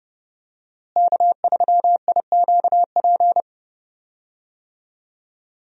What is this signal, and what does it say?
Morse code, heard as a keyed tone, about 30 wpm, 700 Hz. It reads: K3IQP
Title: USA Call Signs - 5 Characters - Repeated 3x - 30wpm